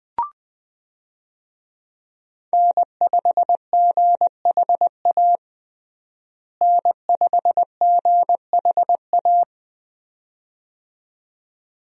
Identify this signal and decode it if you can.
Morse code, heard as a keyed tone, about 21 wpm, 700 Hz. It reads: N5GHA N5GHA
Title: USA Call Signs - 5 Characters - Repeated 2x - 20wpm